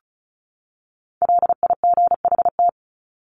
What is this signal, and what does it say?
Morse code, heard as a keyed tone, about 34 wpm, 700 Hz. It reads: LIGHT
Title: Top 300 Words 35wpm